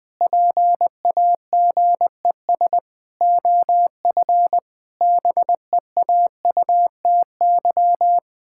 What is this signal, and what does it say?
Morse code, heard as a keyed tone, about 20 wpm, 700 Hz. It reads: PAGES OF BEAUTY
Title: Pangram Challenge v2 - 20wpm